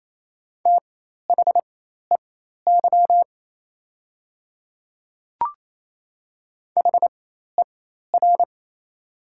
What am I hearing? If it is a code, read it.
Morse code, heard as a keyed tone, about 28 wpm, spaced as for 13 wpm, 700 Hz. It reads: THEY HER
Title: 4x Character Spacing - Top 200 Words - 28wpm